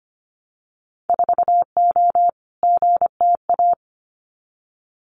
Morse code, heard as a keyed tone, about 25 wpm, 700 Hz. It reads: 4O GTA
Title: Winter Field Day Exchanges 25wpm